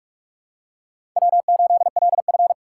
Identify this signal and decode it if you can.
Morse code, heard as a keyed tone, about 45 wpm, 700 Hz. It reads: W8LF
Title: USA Call Signs - 2x Character Spacing - 45wpm